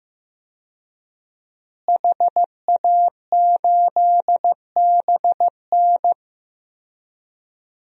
Morse code, heard as a keyed tone, about 15 wpm, 700 Hz. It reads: HA8BN
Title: DXCC-100 Call Signs - 5 Characters - Repeated 2x - 15wpm